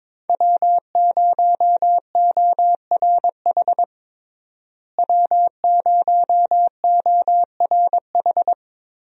Morse code, heard as a keyed tone, about 22 wpm, 700 Hz. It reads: W0ORH W0ORH
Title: USA Call Signs - 5 Characters - Repeated 3x - 22wpm